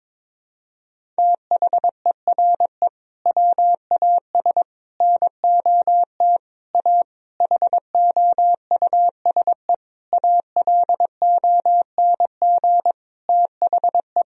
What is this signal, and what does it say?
Morse code, heard as a keyed tone, about 21 wpm, 700 Hz. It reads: THERE WAS NOT A HOUSE ALONG THE
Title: Sentences from Top 1000 Words - Review Part 2 - 22wpm